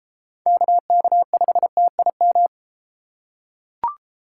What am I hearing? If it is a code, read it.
Morse code, heard as a keyed tone, about 32 wpm, 700 Hz. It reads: KK5TIM